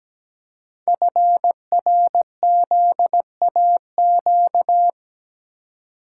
Morse code, heard as a keyed tone, about 17 wpm, 700 Hz. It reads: FRZAQ
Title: Code Groups - 5 Letters - 17wpm